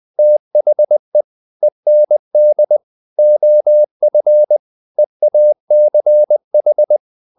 Morse code, heard as a keyed tone, about 20 wpm, 600 Hz. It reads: THE END OF EACH
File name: the old man and the sea / the old man and the sea part 1 20wpm